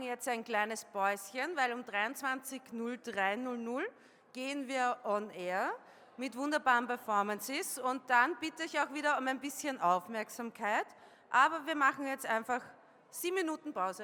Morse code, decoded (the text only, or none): none